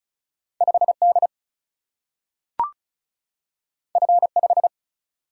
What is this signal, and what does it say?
Morse code, heard as a keyed tone, about 36 wpm, 700 Hz. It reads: 5D F5
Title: Single-Letter Single-Number 35wpm